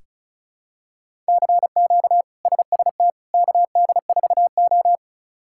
Morse code, heard as a keyed tone, about 36 wpm, 700 Hz. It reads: CQ SST KD4O
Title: K1USN SST QSO Practice - 35wpm